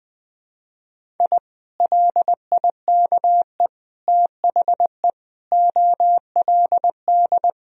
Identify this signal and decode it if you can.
Morse code, heard as a keyed tone, about 21 wpm, 700 Hz. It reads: I LIKE THE OLD
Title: Sentences from Top 1000 Words - Review Part 2 - 20wpm